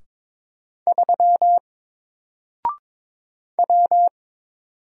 Morse code, heard as a keyed tone, about 22 wpm, 700 Hz. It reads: 3 W
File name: Single-Letter-Number-22wpm